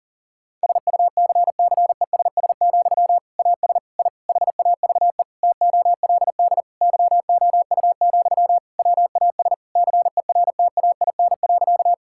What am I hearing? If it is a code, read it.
Morse code, heard as a keyed tone, about 40 wpm, 700 Hz. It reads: SUCCESS, AS I HAVE TOLD YOU, WAS CERTAIN.
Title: Sentences from Top 1000 Words - Review Part 6 - 40wpm